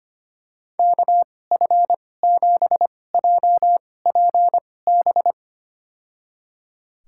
Morse code, heard as a keyed tone, about 25 wpm, spaced as for 18 wpm, 700 Hz. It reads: KF7JPB
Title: USA Call Signs - 2x Character Spacing - 25wpm